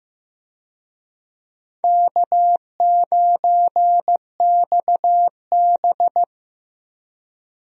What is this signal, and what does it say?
Morse code, heard as a keyed tone, about 15 wpm, 700 Hz. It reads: K9XB
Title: USA Call Signs - 4 Characters - 15wpm